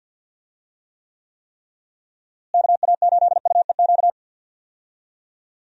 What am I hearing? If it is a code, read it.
Morse code, heard as a keyed tone, about 50 wpm, 700 Hz. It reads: KA8UEQ
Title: USA Call Signs - 6 Characters - Repeated 3x - 50wpm